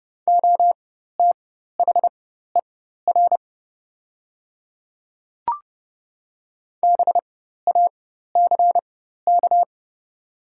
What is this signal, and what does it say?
Morse code, heard as a keyed tone, about 30 wpm, spaced as for 14 wpm, 700 Hz. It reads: OTHER BACK
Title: 4x Character Spacing - Top 200 Words - 30wpm